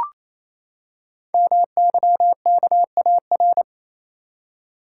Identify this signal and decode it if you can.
Morse code, heard as a keyed tone, about 27 wpm, 700 Hz. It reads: MYKAR
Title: Code Groups - 5 Letters - 28wpm